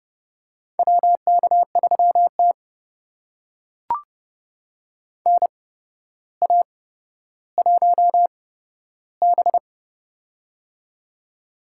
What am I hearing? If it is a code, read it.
Morse code, heard as a keyed tone, about 30 wpm, spaced as for 13 wpm, 700 Hz. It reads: WK3T NA1B
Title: USA Call Signs - 4 Characters - 8x Character Spacing - 30wpm